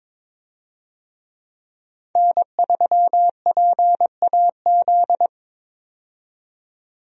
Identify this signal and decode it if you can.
Morse code, heard as a keyed tone, about 22 wpm, 700 Hz. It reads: N3PAZ
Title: USA Call Signs - 5 Characters - 22wpm